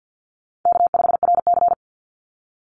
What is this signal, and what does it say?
Morse code, heard as a keyed tone, about 50 wpm, 700 Hz. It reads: K5RC
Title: Call Signs Repeated 3x - 50wpm